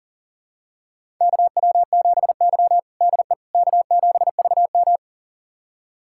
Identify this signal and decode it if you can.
Morse code, heard as a keyed tone, about 37 wpm, 700 Hz. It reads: KW7Y DE K7VM
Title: USA Callsign DE Callsign - 4 Characters - 40wpm